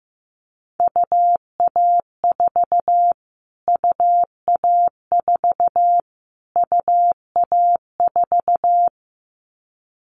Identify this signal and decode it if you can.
Morse code, heard as a keyed tone, about 15 wpm, 700 Hz. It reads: UA4 UA4 UA4
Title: Callsign Prefixes for Top 25 DXCC - Repeated 3x - 15wpm